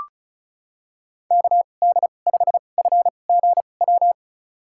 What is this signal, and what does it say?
Morse code, heard as a keyed tone, about 36 wpm, spaced as for 25 wpm, 700 Hz. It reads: KD5FGW
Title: USA Call Signs - 2x Character Spacing - 35wpm